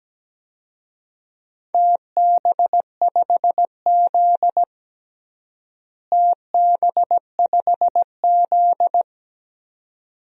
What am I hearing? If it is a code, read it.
Morse code, heard as a keyed tone, about 17 wpm, 700 Hz. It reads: TB5Z TB5Z
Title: DXCC-100 Call Signs - 4 Characters - Repeated 2x - 17wpm